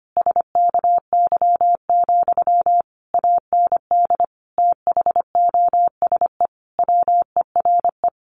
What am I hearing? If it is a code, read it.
Morse code, heard as a keyed tone, about 25 wpm, 700 Hz. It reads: SKY, AND THOSE WERE
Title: A Sound Of Thunder 25wpm